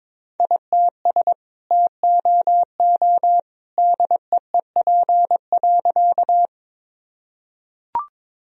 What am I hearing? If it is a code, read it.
Morse code, heard as a keyed tone, about 22 wpm, 700 Hz. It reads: ITS TOO DEEP.
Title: Sentences from Top 1000 Words - Review Part 3 - 22wpm